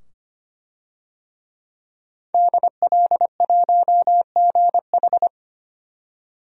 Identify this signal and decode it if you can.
Morse code, heard as a keyed tone, about 25 wpm, 700 Hz. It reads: DL1GH
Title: DXCC-100 Call Signs - 5 Characters - 25wpm